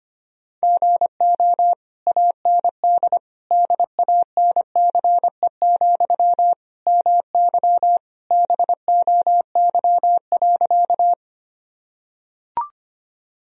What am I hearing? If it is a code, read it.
Morse code, heard as a keyed tone, about 25 wpm, 700 Hz. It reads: GO AND DANCE, MY BOY.
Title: Sentences from Top 1000 Words - Review Part 4 - 25wpm